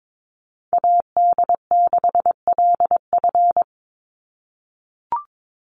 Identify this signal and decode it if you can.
Morse code, heard as a keyed tone, about 22 wpm, 700 Hz. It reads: AD6LF
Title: USA Call Signs 22wpm